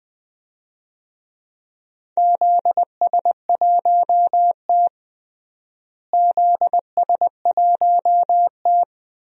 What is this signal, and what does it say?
Morse code, heard as a keyed tone, about 20 wpm, 700 Hz. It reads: ZS1T ZS1T
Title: DXCC-100 Call Signs - 4 Characters - Repeated 3x - 20wpm